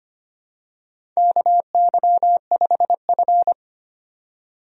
Morse code, heard as a keyed tone, about 25 wpm, 700 Hz. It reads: KY5F
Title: USA Call Signs - 4 Characters - 25wpm